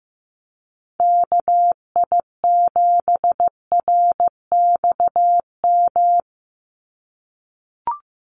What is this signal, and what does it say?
Morse code, heard as a keyed tone, about 15 wpm, 700 Hz. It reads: KI7RXM
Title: USA Call Signs 15wpm